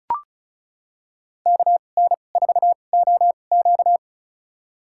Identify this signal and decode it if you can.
Morse code, heard as a keyed tone, about 34 wpm, spaced as for 25 wpm, 700 Hz. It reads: KN4OQ
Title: USA Call Signs - 5 Characters - 2x Character Spacing - 35wpm